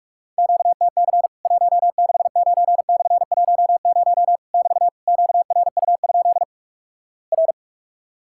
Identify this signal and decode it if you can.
Morse code, heard as a keyed tone, about 46 wpm, 700 Hz. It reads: QTC 169/10 <BT> QRU? R
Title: Worked All Europe DX Contest - QTC Exchange - 45wpm